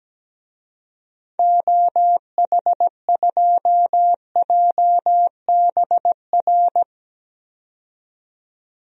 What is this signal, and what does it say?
Morse code, heard as a keyed tone, about 17 wpm, 700 Hz. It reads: OH2JBR
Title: DXCC-100 Call Signs - 6 Characters - 17wpm